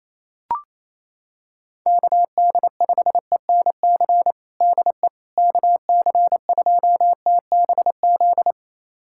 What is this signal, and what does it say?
Morse code, heard as a keyed tone, about 28 wpm, 700 Hz. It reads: KD5ENC DE KC2TBZ
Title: USA Callsign DE Callsign - 6 Characters - 28wpm